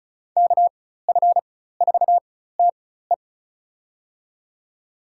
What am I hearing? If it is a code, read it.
Morse code, heard as a keyed tone, about 36 wpm, spaced as for 16 wpm, 700 Hz. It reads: KF4TE
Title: USA Call Signs - 5 Characters - 4x Character Spacing - 35wpm